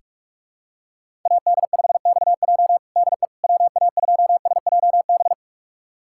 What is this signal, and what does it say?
Morse code, heard as a keyed tone, about 46 wpm, 700 Hz. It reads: AD5XJ DE WA2SJB
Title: USA Callsign DE Callsign 45wpm